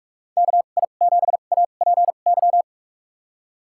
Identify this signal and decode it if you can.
Morse code, heard as a keyed tone, about 46 wpm, spaced as for 33 wpm, 700 Hz. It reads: KI7APY
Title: USA Call Signs - 2x Character Spacing - 45wpm